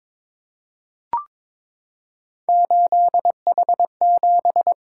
Morse code, heard as a keyed tone, about 22 wpm, 700 Hz. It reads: 8H7
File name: practice / 22wpm / Three Letter-Number Combinations - 22wpm